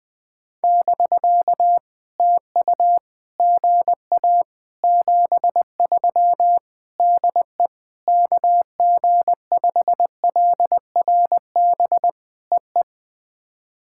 Morse code, heard as a keyed tone, about 20 wpm, 700 Hz. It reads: <BK> TU GA 73 DE KG5LRB EE
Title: POTA QSO Practice - 20wpm